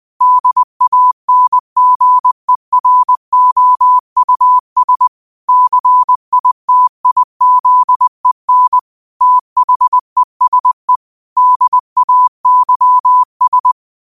Morse code, heard as a keyed tone, about 20 wpm, 1000 Hz. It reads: DANGEROUS CITIZEN THESE DAYS